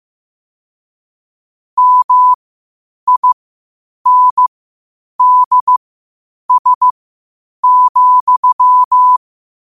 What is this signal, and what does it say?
Morse code, heard as a keyed tone, about 15 wpm, spaced as for 8 wpm, 1000 Hz. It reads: MINDS,
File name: qotdfiles / 8wpm